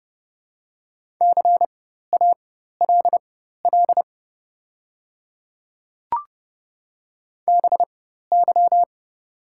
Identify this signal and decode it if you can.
Morse code, heard as a keyed tone, about 30 wpm, spaced as for 14 wpm, 690 Hz. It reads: CALL BY
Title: 4x Character Spacing - Top 100 Words - 30wpm